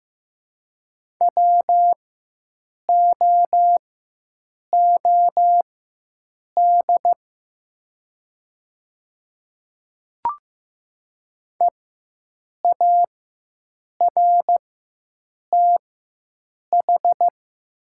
Morse code, heard as a keyed tone, about 15 wpm, spaced as for 7 wpm, 700 Hz. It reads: WOOD EARTH
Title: 4x Character Spacing - Top 500 Words - 15wpm